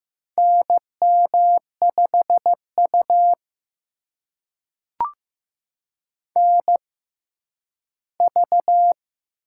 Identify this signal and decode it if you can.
Morse code, heard as a keyed tone, about 15 wpm, spaced as for 10 wpm, 700 Hz. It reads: NM5U NV
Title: USA Call Signs - 4 Characters - 6x Character Spacing - 15wpm